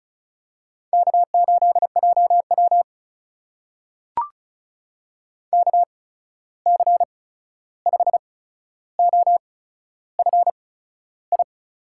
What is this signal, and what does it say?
Morse code, heard as a keyed tone, about 36 wpm, spaced as for 13 wpm, 700 Hz. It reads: K8JW KC5OFI
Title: USA Call Signs - 8x Character Spacing - 35wpm